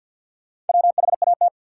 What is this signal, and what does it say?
Morse code, heard as a keyed tone, about 53 wpm, 700 Hz. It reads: WHAT
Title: Sentences from Top 100 Words - New Word Drill - 50wpm